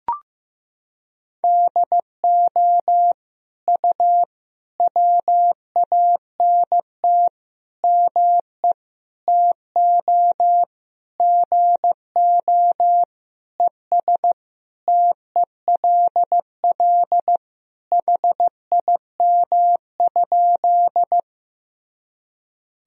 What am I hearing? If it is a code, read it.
Morse code, heard as a keyed tone, about 15 wpm, 700 Hz. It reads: DO U WANT ME TO GO ES TELL HIM?
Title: Sentences from Top 200 Words and Abbreviations 15wpm